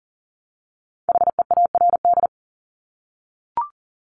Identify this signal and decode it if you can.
Morse code, heard as a keyed tone, about 38 wpm, 690 Hz. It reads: HEARD
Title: Multi-letter Phonemes 40wpm